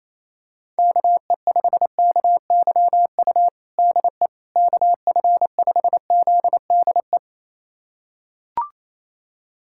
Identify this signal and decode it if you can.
Morse code, heard as a keyed tone, about 28 wpm, 700 Hz. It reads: KE5KYU DE KF5ZDE
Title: USA Callsign DE Callsign 28wpm